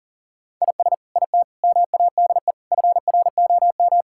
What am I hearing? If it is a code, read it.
Morse code, heard as a keyed tone, about 40 wpm, 700 Hz. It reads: IS IT MADE FROM
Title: Sentences from Top 1600 Words 40wpm